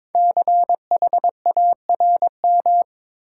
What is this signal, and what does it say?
Morse code, heard as a keyed tone, about 22 wpm, 700 Hz. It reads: CHARM
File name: 5 Letter Words - First 3 Letters Given - 22wpm